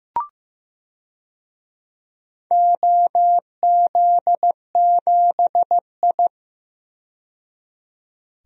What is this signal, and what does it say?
Morse code, heard as a keyed tone, about 15 wpm, 700 Hz. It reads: OZ7I
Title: DXCC-100 Call Signs - 4 Characters - 15wpm